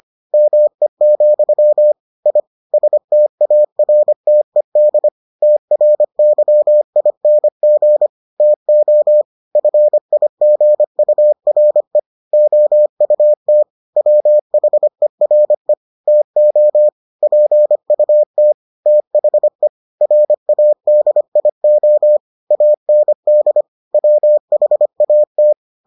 Morse code, heard as a keyed tone, about 25 wpm, 600 Hz. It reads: ME, I STARTED TRYING TO FIGURE OUT WHERE TO PUT THE RADIO AND WHAT